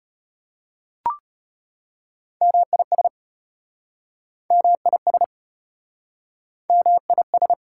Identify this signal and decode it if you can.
Morse code, heard as a keyed tone, about 33 wpm, 700 Hz. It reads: MIS MIS MIS